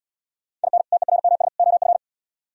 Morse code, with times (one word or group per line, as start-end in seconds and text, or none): 0.6-0.8 IN
0.9-1.5 TERMS
1.6-2.0 OF